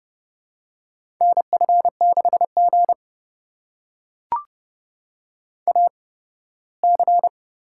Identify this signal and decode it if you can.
Morse code, heard as a keyed tone, about 31 wpm, spaced as for 18 wpm, 700 Hz. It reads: NF6G AC
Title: USA Call Signs - 4 Characters - 8x Character Spacing - 30wpm